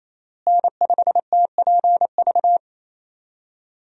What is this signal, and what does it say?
Morse code, heard as a keyed tone, about 28 wpm, 700 Hz. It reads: N5TPV
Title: USA Call Signs - 5 Characters - Repeated 2x - 28wpm